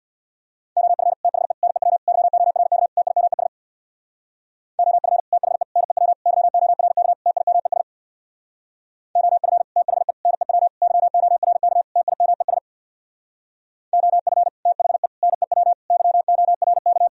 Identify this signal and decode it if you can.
Morse code, heard as a keyed tone, about 64 wpm, 700 Hz. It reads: OF THE NEW YORK TIMES OF THE NEW YORK TIMES OF THE NEW YORK TIMES OF THE NEW YORK